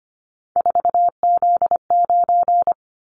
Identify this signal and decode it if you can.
Morse code, heard as a keyed tone, about 25 wpm, 700 Hz. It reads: 4Z9
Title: Callsign Prefixes for Top 50 DXCC - 25wpm